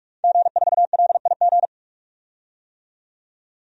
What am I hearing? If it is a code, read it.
Morse code, heard as a keyed tone, about 42 wpm, 700 Hz. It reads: G4LIG
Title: DXCC-100 Call Signs - 5 Characters - 45wpm